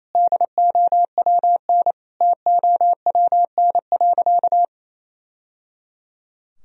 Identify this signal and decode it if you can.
Morse code, heard as a keyed tone, about 28 wpm, 700 Hz. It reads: DOWN TOWN.
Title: Sentences from Top 1000 Words - Review Part 4 - 28wpm